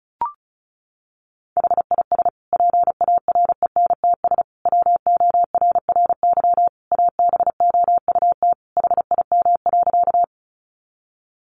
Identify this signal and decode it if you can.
Morse code, heard as a keyed tone, about 34 wpm, 700 Hz. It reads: HIS PARENTS WORRY ABOUT HIM.
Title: Easy Phrases 35wpm